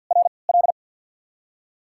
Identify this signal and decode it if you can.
Morse code, heard as a keyed tone, about 53 wpm, 700 Hz. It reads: R L
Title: USA Call Signs - 5 Characters - 3x Character Spacing - 50wpm